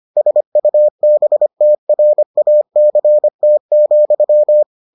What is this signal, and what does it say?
Morse code, heard as a keyed tone, about 25 wpm, 600 Hz. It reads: SUBTRACT,